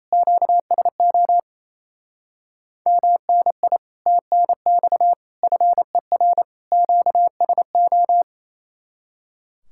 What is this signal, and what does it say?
Morse code, heard as a keyed tone, about 36 wpm, 700 Hz. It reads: QSO MNI TNX FER QSO